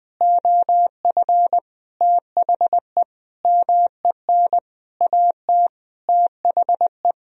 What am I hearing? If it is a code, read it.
Morse code, heard as a keyed tone, about 20 wpm, 700 Hz. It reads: OF THE MEN AT THE
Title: Sentences from Top 1000 Words - Review Part 2 - 20wpm